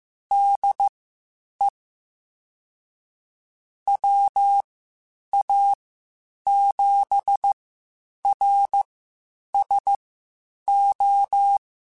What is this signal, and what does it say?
Morse code, heard as a keyed tone, about 15 wpm, spaced as for 8 wpm, 800 Hz. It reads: DE WA7RSO
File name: (spaced) / QSOs Spaced-mp3s / WA7QOX 2 @13 sp